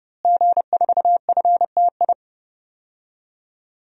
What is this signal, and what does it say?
Morse code, heard as a keyed tone, about 32 wpm, 700 Hz. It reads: G4FTI
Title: DXCC-100 Call Signs - 5 Characters - Repeated 3x - 30wpm